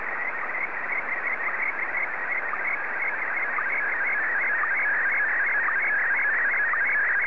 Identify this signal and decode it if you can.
SSTV (slow-scan television), Scottie DX.
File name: sstv / SDX-VK2GHO